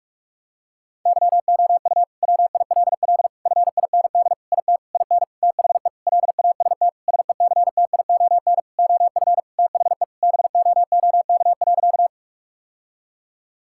Morse code, heard as a keyed tone, about 46 wpm, 700 Hz. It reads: YOU WILL FIND IT IN THE LAST SECTION OF THE BOOK.
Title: Sentences from Top 1000 Words - Review Part 5 - 45wpm